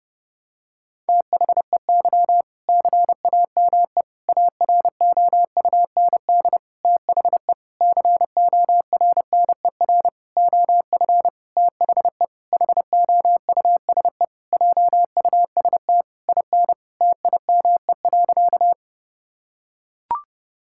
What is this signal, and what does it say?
Morse code, heard as a keyed tone, about 32 wpm, 700 Hz. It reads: THEY CAME AROUND THE CORNER OF THE HOUSE JUST IN TIME.
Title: Sentences from Top 1600 Words 30wpm